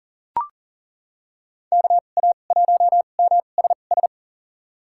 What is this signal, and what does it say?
Morse code, heard as a keyed tone, about 39 wpm, spaced as for 29 wpm, 700 Hz. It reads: KA1MSS
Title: USA Call Signs - 6 Characters - 2x Character Spacing - 40wpm